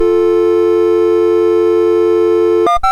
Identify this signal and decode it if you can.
Dial tone (North America), 350 + 440 Hz, with noise.
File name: Misc / SoundEffects / Dial Tone T001.mus MOS6581R4